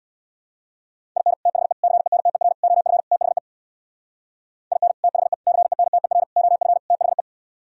Morse code, heard as a keyed tone, about 64 wpm, 700 Hz. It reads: IN THE CENTER OF THE IN THE CENTER OF THE